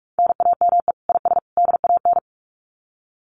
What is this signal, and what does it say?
Morse code, heard as a keyed tone, about 44 wpm, 700 Hz. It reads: NAME IS DAN
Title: Common QSO Phrases 45wpm